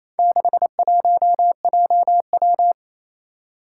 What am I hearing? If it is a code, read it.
Morse code, heard as a keyed tone, about 28 wpm, 700 Hz. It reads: 61JW
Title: Four Challenging Letter-Number Combinations 28wpm